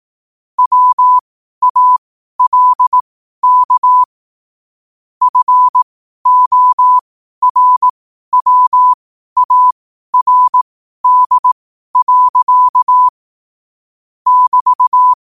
Morse code, heard as a keyed tone, about 18 wpm, spaced as for 12 wpm, 1000 Hz. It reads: WALK FORWARD. <BT>
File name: qotdfiles / 12wpm